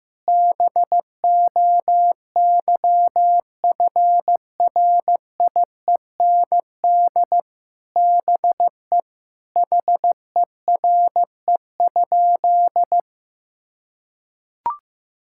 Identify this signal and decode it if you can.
Morse code, heard as a keyed tone, about 15 wpm, 700 Hz. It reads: BOYFRIEND BE HERE?
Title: Sentences from Top 1000 Words - Review Part 5 - 15wpm